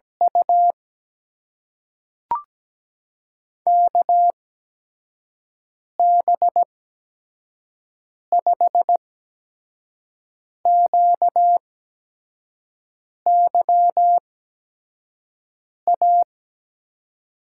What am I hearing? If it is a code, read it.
Morse code, heard as a keyed tone, about 17 wpm, spaced as for 5 wpm, 700 Hz. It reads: U KB5QYA
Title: USA Call Signs - 6 Characters - 8x Character Spacing - 17wpm